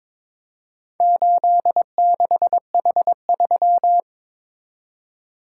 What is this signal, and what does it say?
Morse code, heard as a keyed tone, about 23 wpm, 700 Hz. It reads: 86H3
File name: practice / 22wpm / Four Letter-Number Combinations - 22wpm